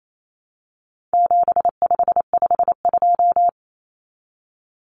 Morse code, heard as a keyed tone, about 28 wpm, 700 Hz. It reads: 7552
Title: Common Cut Numbers - 4-Digit 28wpm